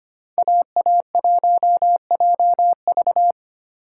Morse code, heard as a keyed tone, about 34 wpm, spaced as for 30 wpm, 700 Hz. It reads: AA1JV
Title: Speed Racing - USA 2x2 Call Signs 25wpm